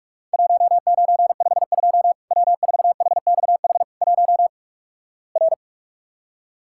Morse code, heard as a keyed tone, about 46 wpm, 700 Hz. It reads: 1952 W4HXH 1 R
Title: Worked All Europe DX Contest - QTC Exchange - 45wpm